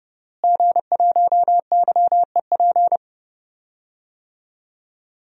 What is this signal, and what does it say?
Morse code, heard as a keyed tone, about 29 wpm, 700 Hz. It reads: G1YEP
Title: DXCC-100 Call Signs - 5 Characters - Repeated 2x - 30wpm